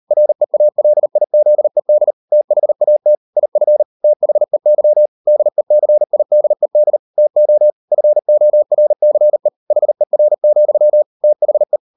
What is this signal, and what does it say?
Morse code, heard as a keyed tone, about 38 wpm, 600 Hz. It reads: REALIZED THAT IF THEY DECIDED TO FORCE HER, THE